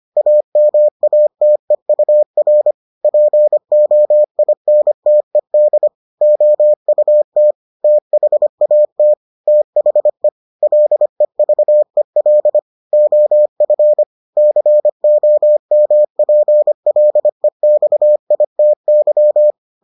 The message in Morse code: AMATEUR POINTED OUT THAT THE LEVEL OF COMPLEXITY

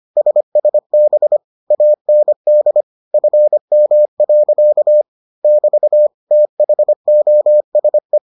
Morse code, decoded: SSB AND FM. <BT> THOSE

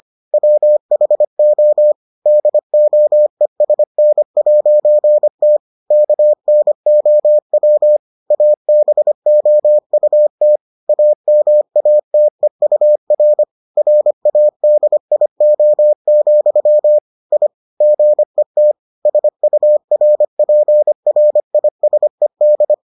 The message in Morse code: WHO DOESN'T KNOW ABOUT AMATEUR RADIO, I GET SURPRISED

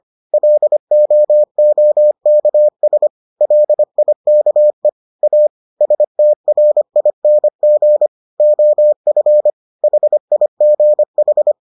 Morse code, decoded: LOOKS LIKE A STRING OF HIGH

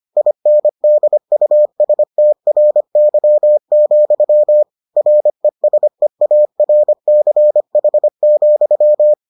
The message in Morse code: INDUSTRY, RESEARCH,

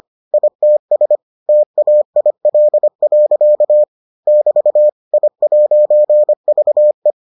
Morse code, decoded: ITS TAIL. <BT> I'VE

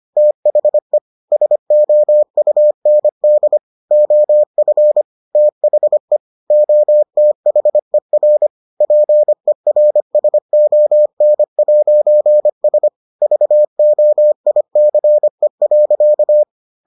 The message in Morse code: THE SOUND OF THE OTHER PERSON'S VOICE.

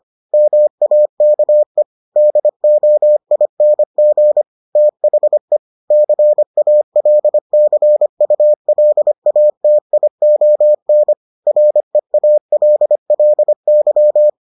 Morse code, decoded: MAKE DOING THE CALCULATION REALLY

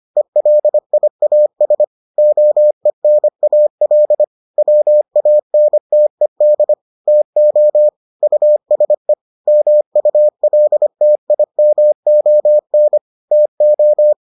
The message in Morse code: ELIAS OENAL WANTED TO USE MULTIMON TO

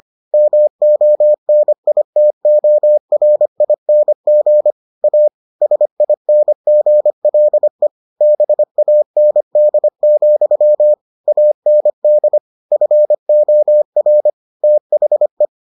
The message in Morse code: MONITORING A SINGLE BAND, AND FOR THE

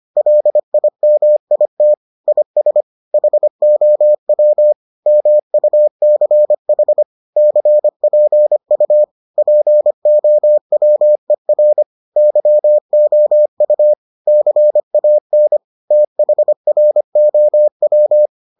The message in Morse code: LIMIT IS HOW MUCH CPU POWER YOU CAN THROW

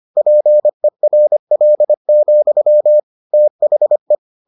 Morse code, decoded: PERL, THE

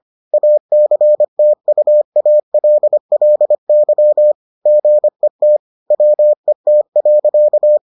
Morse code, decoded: ACTUALLY GET WET.

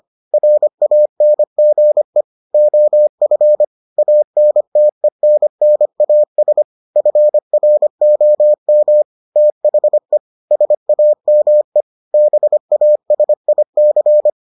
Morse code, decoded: RANGE OF ANTENNAS FROM THE SAME BASIC

